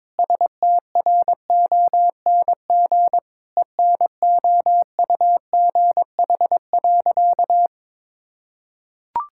Morse code, heard as a keyed tone, about 21 wpm, 700 Hz. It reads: STRONG ENOUGH.
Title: Sentences from Top 1600 Words 22wpm